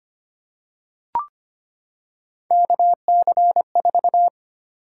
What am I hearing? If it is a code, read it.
Morse code, heard as a keyed tone, about 25 wpm, 700 Hz. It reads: KC4
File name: Three Letter-Number Combinations - 25wpm